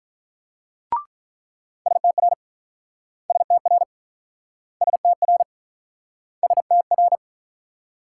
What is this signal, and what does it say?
Morse code, heard as a keyed tone, about 48 wpm, 700 Hz. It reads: STR STR STR STR